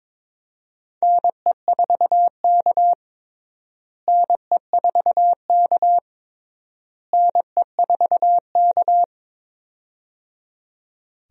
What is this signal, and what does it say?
Morse code, heard as a keyed tone, about 21 wpm, 700 Hz. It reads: NE4K NE4K NE4K
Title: USA Call Signs - 4 Characters - Repeated 3x - 22wpm